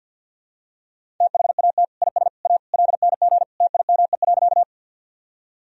Morse code, heard as a keyed tone, about 49 wpm, 700 Hz. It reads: THAT IS A LNG TIME.
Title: Sentences from Top 100 Words and Abbreviations 50wpm